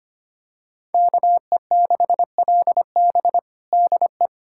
Morse code, heard as a keyed tone, about 25 wpm, 700 Hz. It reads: KE6LB DE
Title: USA Callsign DE Callsign - 5 Characters - 25wpm